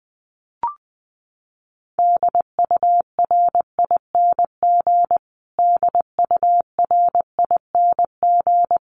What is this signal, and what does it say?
Morse code, heard as a keyed tone, about 19 wpm, 700 Hz. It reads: DURING DURING
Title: Top 500 Words Repeated 20wpm